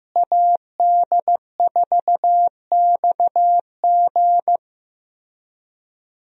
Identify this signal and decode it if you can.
Morse code, heard as a keyed tone, about 16 wpm, 700 Hz. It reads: AD4XG
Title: USA Call Signs - 5 Characters - Repeated 2x - 15wpm